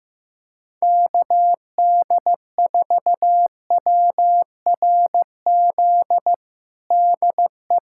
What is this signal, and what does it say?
Morse code, heard as a keyed tone, about 15 wpm, 700 Hz. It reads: KD4WRZ DE
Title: USA Callsign DE Callsign - 6 Characters - 15wpm